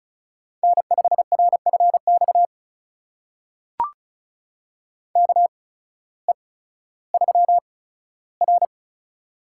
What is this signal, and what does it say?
Morse code, heard as a keyed tone, about 34 wpm, spaced as for 16 wpm, 700 Hz. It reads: N5RFX KE3R
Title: USA Call Signs - 8x Character Spacing - 35wpm